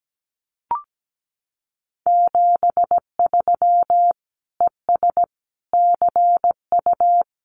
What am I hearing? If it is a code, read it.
Morse code, heard as a keyed tone, about 17 wpm, 700 Hz. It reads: 73 ES CU
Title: Common QSO Phrases Encore 17wpm